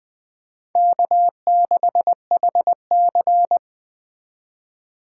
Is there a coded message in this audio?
Yes: Morse code from 0.8 to 3.6 s, heard as a keyed tone, about 20 wpm, 700 Hz. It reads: K6HC